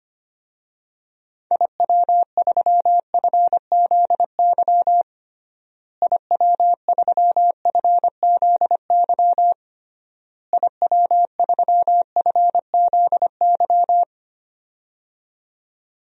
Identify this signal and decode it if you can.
Morse code, heard as a keyed tone, about 25 wpm, 700 Hz. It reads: IW3FZY IW3FZY IW3FZY
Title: DXCC-100 Call Signs - 6 Characters - Repeated 3x - 25wpm